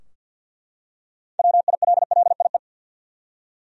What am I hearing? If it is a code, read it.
Morse code, heard as a keyed tone, about 52 wpm, 700 Hz. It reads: WILLIE
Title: USA Common Names - 50wpm